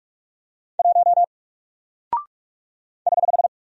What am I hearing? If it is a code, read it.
Morse code, heard as a keyed tone, about 46 wpm, 700 Hz. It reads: 1 <HH>
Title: Single Character (Letter:Number:Punctuation:Prosign) Alt 45wpm